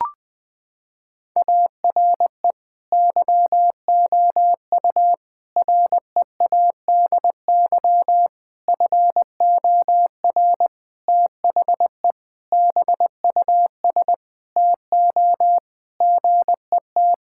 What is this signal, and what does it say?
Morse code, heard as a keyed tone, about 20 wpm, 700 Hz. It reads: ARE YOU READY FOR THE BUS TO GET
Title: Sentences from Top 1000 Words - Review Part 6 - 20wpm